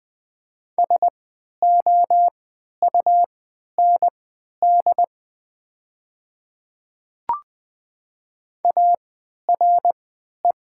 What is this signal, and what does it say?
Morse code, heard as a keyed tone, about 20 wpm, spaced as for 11 wpm, 700 Hz. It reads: SOUND ARE
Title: 3x Character Spacing - Top 100 Words - 20wpm